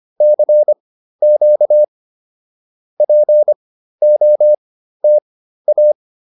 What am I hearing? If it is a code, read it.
Morse code, heard as a keyed tone, about 25 wpm, spaced as for 13 wpm, 590 Hz. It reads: CQ POTA